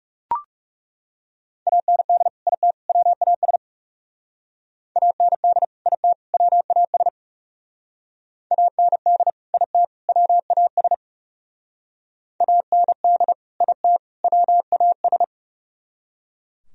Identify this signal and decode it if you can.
Morse code, heard as a keyed tone, about 37 wpm, 700 Hz. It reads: AND IT WAS AND IT WAS AND IT WAS AND IT WAS
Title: Speed-Racing - Sets of 3 Words 30wpm